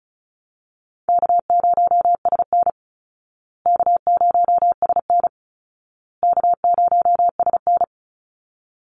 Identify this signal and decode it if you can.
Morse code, heard as a keyed tone, about 34 wpm, 700 Hz. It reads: K0SN K0SN K0SN
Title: Call Signs Repeated 2x - 35wpm